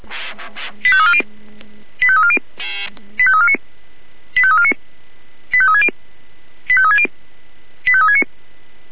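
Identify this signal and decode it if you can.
Five-tone selective call (ZVEI), heard as quick runs of five tones, tones of about 70 ms. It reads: 85340 85340 85349 85349 85360 85360 85369
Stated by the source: Five-tone ZVEI selective call received by SDR.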